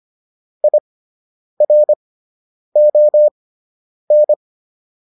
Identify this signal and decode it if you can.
Morse code, heard as a keyed tone, about 25 wpm, spaced as for 9 wpm, 600 Hz. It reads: IRON